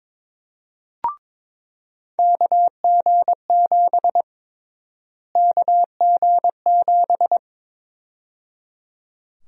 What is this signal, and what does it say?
Morse code, heard as a keyed tone, about 22 wpm, 700 Hz. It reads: KG7 KG7
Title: USA Call Sign Prefixes - Repeated 2x - 22wpm